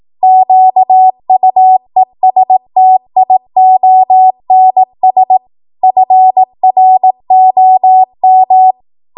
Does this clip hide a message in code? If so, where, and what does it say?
Yes: Morse code from 0.2 to 8.7 s, heard as a keyed tone, about 18 wpm, 750 Hz. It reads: QUESTIONS FROM